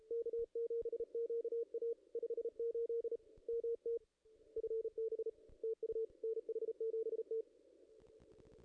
Morse code, heard as a keyed tone, about 33 wpm, 440 Hz. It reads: K7QA 58 MT FB TU NH7T
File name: contests / ss-cw-k5tr-op-n5rz -2200